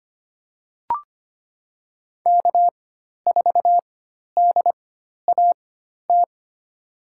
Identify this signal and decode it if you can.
Morse code, heard as a keyed tone, about 25 wpm, spaced as for 12 wpm, 700 Hz. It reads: K4DAT
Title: USA Call Signs - 4x Character Spacing - 25wpm